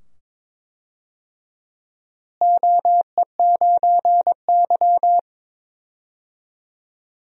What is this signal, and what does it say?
Morse code, heard as a keyed tone, about 22 wpm, 700 Hz. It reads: OE9Y
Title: DXCC-100 Call Signs - 4 Characters - Repeated 2x - 22wpm